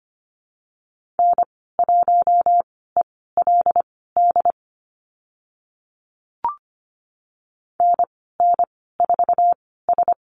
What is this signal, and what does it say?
Morse code, heard as a keyed tone, about 25 wpm, spaced as for 16 wpm, 700 Hz. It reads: N1ELD NN4S
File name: Lesson 15g - Number 5 - Call Signs 25wpm